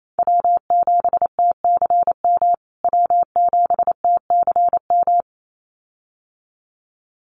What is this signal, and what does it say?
Morse code, heard as a keyed tone, about 28 wpm, 700 Hz. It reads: W7TCM W7TCM
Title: USA Call Signs Repeated 2x 28wpm